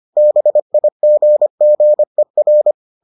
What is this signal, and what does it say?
Morse code, heard as a keyed tone, about 25 wpm, 600 Hz. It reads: BIGGER